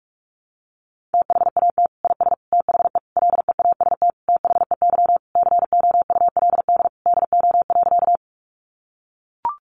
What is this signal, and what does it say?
Morse code, heard as a keyed tone, about 44 wpm, 700 Hz. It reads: THAT IS THE LEAST THEY COULD DO.
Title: Sentences from Top 700 Words 45wpm